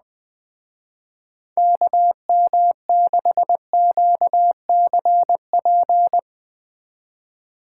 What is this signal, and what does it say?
Morse code, heard as a keyed tone, about 20 wpm, 700 Hz. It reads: KM6QCP
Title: Speed Racing - USA 2x3 Call Signs 17wpm